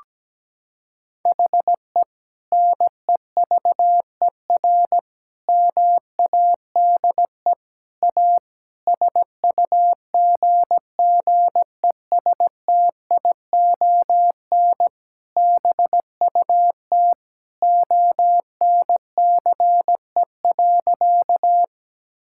Morse code, heard as a keyed tone, about 17 wpm, 700 Hz. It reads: HE NEVER MADE A SUGGESTION BUT ONCE.
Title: Sentences from Top 1600 Words 17wpm